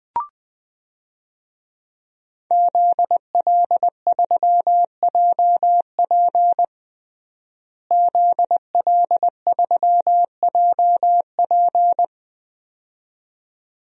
Morse code, heard as a keyed tone, about 20 wpm, 700 Hz. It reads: ZL3JP ZL3JP
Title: DXCC-100 Call Signs - 5 Characters - Repeated 2x - 20wpm